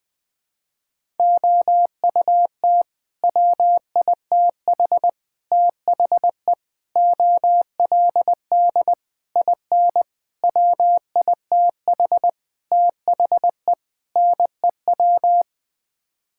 Morse code, heard as a keyed tone, about 20 wpm, 700 Hz. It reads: OUT WITH THE OLD IN WITH THE NEW
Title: Sets of 8 Words - 20wpm